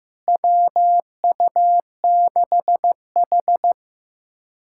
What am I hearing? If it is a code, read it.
Morse code, heard as a keyed tone, about 15 wpm, 700 Hz. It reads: WU6H
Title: USA Callsign DE Callsign - 4 Characters - 15wpm